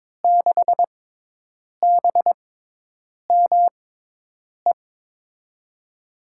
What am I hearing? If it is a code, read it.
Morse code, heard as a keyed tone, about 22 wpm, spaced as for 8 wpm, 700 Hz. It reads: 6BME
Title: USA Call Signs - 5 Characters - 6x Character Spacing - 22wpm